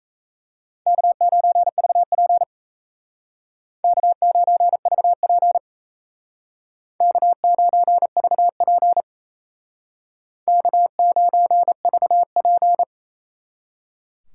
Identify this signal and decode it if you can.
Morse code, heard as a keyed tone, about 35 wpm, 700 Hz. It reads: K9VP K9VP K9VP K9VP